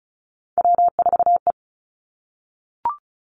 Morse code, heard as a keyed tone, about 36 wpm, 700 Hz. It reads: W4E
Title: Three Challenging Letter-Number Combinations 35wpm